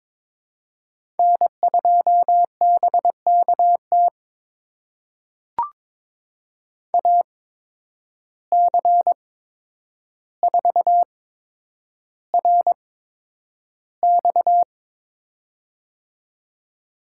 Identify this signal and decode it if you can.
Morse code, heard as a keyed tone, about 22 wpm, spaced as for 9 wpm, 700 Hz. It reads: N2BKT AC4RX
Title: USA Call Signs - 5 Characters - 8x Character Spacing - 22wpm